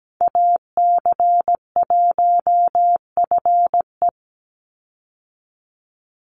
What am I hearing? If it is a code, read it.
Morse code, heard as a keyed tone, about 17 wpm, 700 Hz. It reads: AC1FE
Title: Call Signs - 17wpm